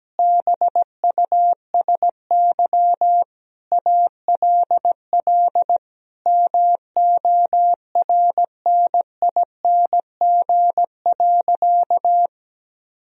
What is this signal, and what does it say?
Morse code, heard as a keyed tone, about 17 wpm, 700 Hz. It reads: BUSY ALL MORNING.
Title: Sentences from Top 1000 Words - Review Part 3 - 17wpm